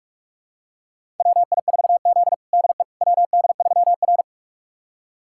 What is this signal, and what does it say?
Morse code, heard as a keyed tone, about 43 wpm, 700 Hz. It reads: WI4Z DE WD3R
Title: USA Callsign DE Callsign - 4 Characters - 45wpm